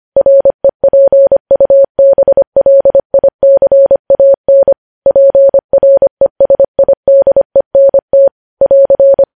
Morse code, heard as a keyed tone, about 25 wpm, 550 Hz. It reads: REPUBLICAN PRESIDENT <AR>
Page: news headlines in morse code at 25 wpm